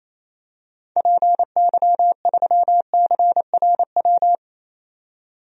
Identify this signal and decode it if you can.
Morse code, heard as a keyed tone, about 28 wpm, 700 Hz. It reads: PY3CRW
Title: DXCC-100 Call Signs - 6 Characters - 28wpm